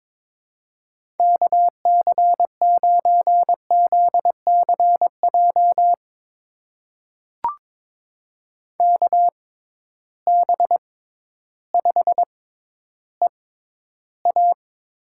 Morse code, heard as a keyed tone, about 21 wpm, spaced as for 12 wpm, 700 Hz. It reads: KC9ZCJ KB5EA
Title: USA Call Signs - 6 Characters - 6x Character Spacing - 22wpm